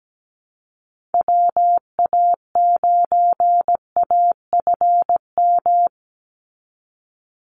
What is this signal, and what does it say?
Morse code, heard as a keyed tone, about 17 wpm, 700 Hz. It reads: WA9AFM
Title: USA Call Signs 17wpm